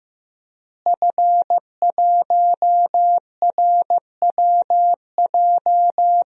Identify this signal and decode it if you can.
Morse code, heard as a keyed tone, about 15 wpm, 700 Hz. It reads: F1RWJ
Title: DXCC-100 Call Signs - 5 Characters - Repeated 3x - 15wpm